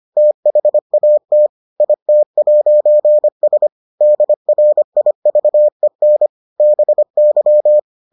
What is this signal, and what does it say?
Morse code, heard as a keyed tone, about 25 wpm, 600 Hz. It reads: THAT IT'S DRIVEN BY